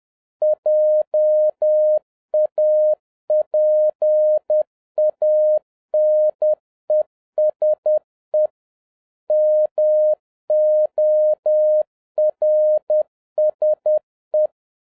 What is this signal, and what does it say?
Morse code, heard as a keyed tone, about 10 wpm, 610 Hz. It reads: JAPANESE MORSE
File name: 2011-07-mp3 / n16 JAPAN TO CANCEL MORSE CODE EXAMINATION 10wpm